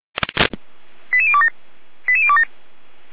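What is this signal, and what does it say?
Five-tone selective call (ZVEI), heard as quick runs of five tones, tones of about 70 ms. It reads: 88027 88027
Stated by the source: ZVEI five-tone selective call received on SDR.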